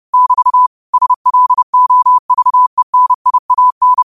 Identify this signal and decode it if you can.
Morse code, heard as a keyed tone, about 30 wpm, 1000 Hz. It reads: <BT> SLOVENIAN
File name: qotdfiles / 30wpm